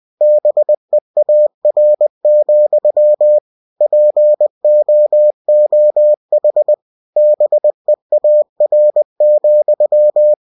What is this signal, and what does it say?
Morse code, heard as a keyed tone, about 20 wpm, 600 Hz. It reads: BEAR, POOH BEAR,
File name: cw-books / winnie-the-pooh / winnie-20wpm-page-0099